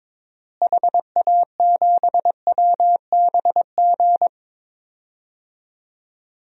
Morse code, heard as a keyed tone, about 22 wpm, 700 Hz. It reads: HA7WBG
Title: DXCC-100 Call Signs - 6 Characters - Repeated 2x - 22wpm